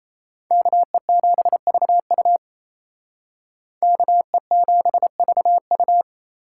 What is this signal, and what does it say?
Morse code, heard as a keyed tone, about 35 wpm, 700 Hz. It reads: KE7VU KE7VU
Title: Speed Racing - USA 2x2 Call Signs 28wpm